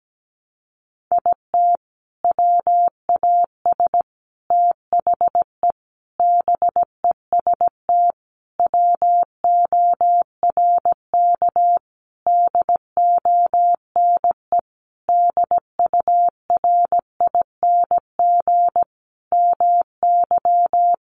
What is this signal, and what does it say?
Morse code, heard as a keyed tone, about 17 wpm, 700 Hz. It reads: IT WAS THE BEST WORK DONE DURING MY
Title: Sentences from Top 1300 Words 17wpm